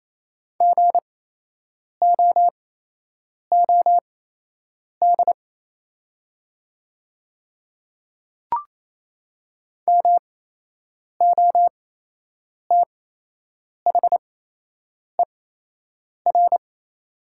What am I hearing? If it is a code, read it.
Morse code, heard as a keyed tone, about 28 wpm, spaced as for 8 wpm, 700 Hz. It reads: GOOD MOTHER
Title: 8x Character Spacing - Top 300 Words - 28wpm